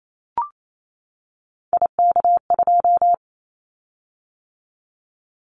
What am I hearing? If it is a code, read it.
Morse code, heard as a keyed tone, about 28 wpm, 700 Hz. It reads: IK2
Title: Callsign Prefixes for Top 50 DXCC - 28wpm